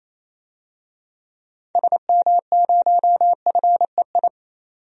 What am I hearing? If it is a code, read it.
Morse code, heard as a keyed tone, about 27 wpm, 700 Hz. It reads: SM0FEI
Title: DXCC-100 Call Signs - 6 Characters - 28wpm